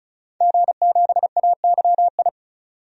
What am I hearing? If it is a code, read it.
Morse code, heard as a keyed tone, about 35 wpm, 700 Hz. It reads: G7AYI
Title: DXCC-100 Call Signs - 5 Characters - 35wpm